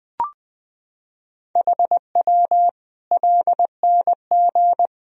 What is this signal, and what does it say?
Morse code, heard as a keyed tone, about 20 wpm, 700 Hz. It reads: HW LNG
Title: Sentences from Top 500 Words and Abbreviations 20wpm